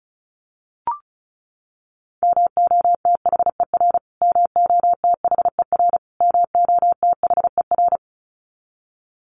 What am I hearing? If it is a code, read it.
Morse code, heard as a keyed tone, about 34 wpm, 700 Hz. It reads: MOTHER MOTHER MOTHER
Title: Top 200 Words Repeated 35wpm